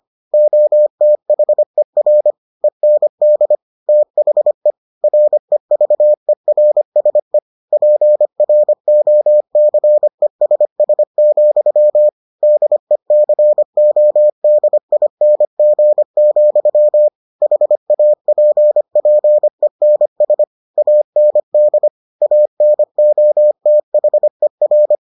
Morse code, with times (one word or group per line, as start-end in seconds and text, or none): 0.3-2.3 OTHER
2.6-3.5 END
3.9-4.7 THE
5.0-7.4 REVERSE
7.7-12.1 PROCESS,
12.4-17.1 DECODING,
17.4-20.4 HAPPENS
20.8-21.9 AND
22.2-24.9 ANOTHER